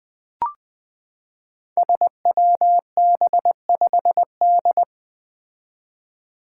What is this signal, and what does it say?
Morse code, heard as a keyed tone, about 20 wpm, 700 Hz. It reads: SWB5D
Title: Five Challenging Letter-Number Combinations 20wpm